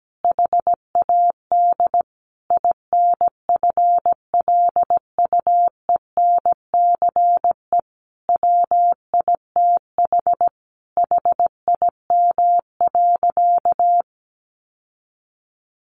Morse code, heard as a keyed tone, about 17 wpm, 700 Hz. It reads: HAD INFLUENCE WITH HIM.
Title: Sentences from Top 1100 Words 17wpm